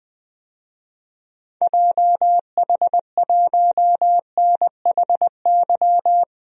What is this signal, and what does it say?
Morse code, heard as a keyed tone, about 20 wpm, 700 Hz. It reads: JH1NHY
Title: DXCC-100 Call Signs - 6 Characters - 20wpm